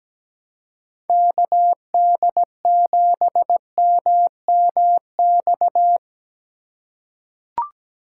Morse code, heard as a keyed tone, about 17 wpm, 700 Hz. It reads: KD7MMX
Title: USA Call Signs - 6 Characters - 3x Character Spacing - 17wpm